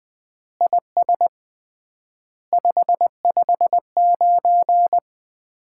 Morse code, heard as a keyed tone, about 20 wpm, 700 Hz. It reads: IS 559
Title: Common QSO Phrases - 3x Word-Spacing - 20wpm